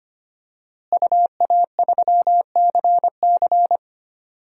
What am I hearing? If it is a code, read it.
Morse code, heard as a keyed tone, about 25 wpm, 700 Hz. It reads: UA3CC
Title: DXCC-100 Call Signs - 5 Characters - Repeated 2x - 25wpm